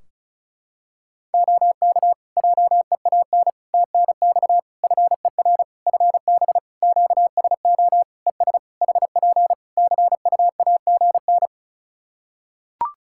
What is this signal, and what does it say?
Morse code, heard as a keyed tone, about 34 wpm, 700 Hz. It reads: OK JEAN TNX FER FB QSO ES HP CUAGN
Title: Practice QSO - Exchange 4 - Ending - 35wpm